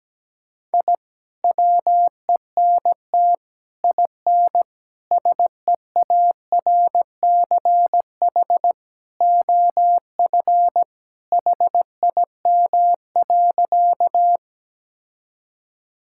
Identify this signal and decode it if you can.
Morse code, heard as a keyed tone, about 17 wpm, 700 Hz. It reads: I WENT IN SEARCH OF HIM.
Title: Sentences from Top 1000 Words - Review Part 5 - 17wpm